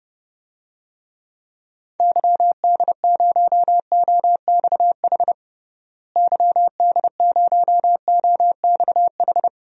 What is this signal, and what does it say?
Morse code, heard as a keyed tone, about 30 wpm, 700 Hz. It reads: YD0OXH YD0OXH
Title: DXCC-100 Call Signs - 6 Characters - Repeated 2x - 30wpm